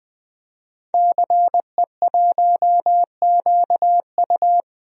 Morse code, heard as a keyed tone, about 20 wpm, 700 Hz. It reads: CE1QU
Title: DXCC-100 Call Signs - 5 Characters - 20wpm